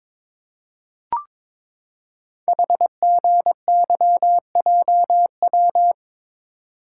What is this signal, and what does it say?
Morse code, heard as a keyed tone, about 22 wpm, 700 Hz. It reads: HGYJW
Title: Code Groups - 5 Letters - 22wpm